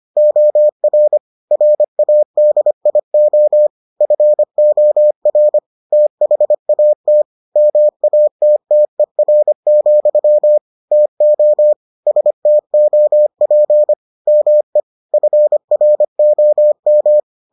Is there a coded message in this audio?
Yes: Morse code from 0.2 to 17.2 s, heard as a keyed tone, about 25 wpm, 600 Hz. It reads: OR RADIO FOR THAT MATTER, TO STOP ME FROM